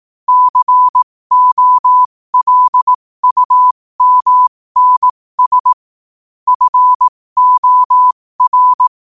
Morse code, heard as a keyed tone, about 18 wpm, spaced as for 15 wpm, 1000 Hz. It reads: COLUMNS FOR
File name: qotdfiles / 15wpm